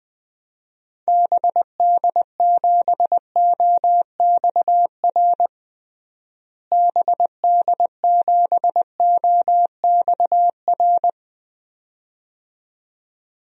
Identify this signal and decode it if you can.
Morse code, heard as a keyed tone, about 20 wpm, 700 Hz. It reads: BD7OXR BD7OXR
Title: DXCC-100 Call Signs - 6 Characters - Repeated 2x - 20wpm